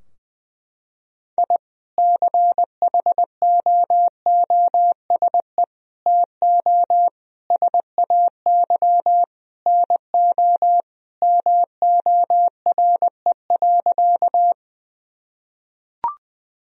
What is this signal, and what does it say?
Morse code, heard as a keyed tone, about 20 wpm, 700 Hz. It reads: I CHOOSE TO SAY NO MORE.
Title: Sentences from Top 1000 Words - Review Part 4 - 20wpm